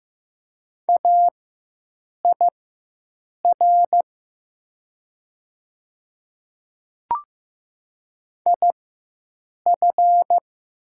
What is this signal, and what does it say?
Morse code, heard as a keyed tone, about 15 wpm, spaced as for 7 wpm, 700 Hz. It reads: AIR IF
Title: 4x Character Spacing - Top 300 Words - 15wpm